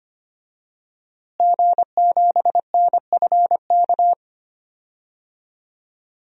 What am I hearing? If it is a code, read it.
Morse code, heard as a keyed tone, about 25 wpm, 700 Hz. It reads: G7NFK
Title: DXCC-100 Call Signs - 5 Characters - Repeated 3x - 25wpm